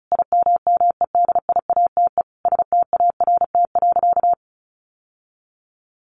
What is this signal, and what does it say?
Morse code, heard as a keyed tone, about 34 wpm, 700 Hz. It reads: IMMEDIATE START.
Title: Sentences from Top 1200 Words 35wpm